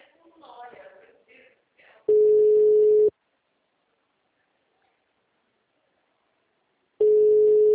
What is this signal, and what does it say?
Ringback tone (Europe, 425 Hz), with speech.